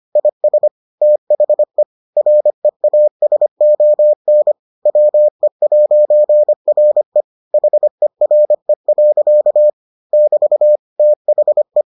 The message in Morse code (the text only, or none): IS THE REASON WE'RE HERE. <BT> THE